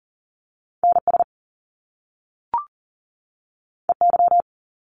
Morse code, heard as a keyed tone, about 41 wpm, 700 Hz. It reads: NS EY
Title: Top Two Letter Combinations in Words 40wpm